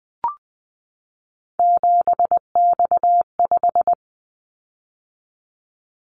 Morse code, heard as a keyed tone, about 20 wpm, 700 Hz. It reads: 7X5
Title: Callsign Prefixes for Top 100 DXCC - 20wpm